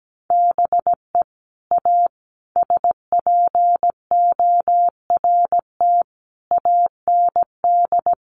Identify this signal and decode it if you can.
Morse code, heard as a keyed tone, about 17 wpm, 700 Hz. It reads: BE A SPORT AND
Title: Sentences from Top 1500 Words 17wpm